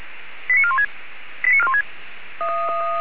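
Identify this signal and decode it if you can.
Five-tone selective call (ZVEI), heard as quick runs of five tones, tones of about 70 ms. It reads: 89416 89416
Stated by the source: ZVEI five-tone selective call received on an SDR.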